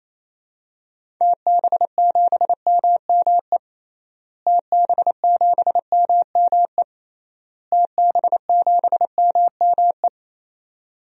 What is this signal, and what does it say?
Morse code, heard as a keyed tone, about 28 wpm, 700 Hz. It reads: TB7MME TB7MME TB7MME
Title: DXCC-100 Call Signs - 6 Characters - Repeated 3x - 28wpm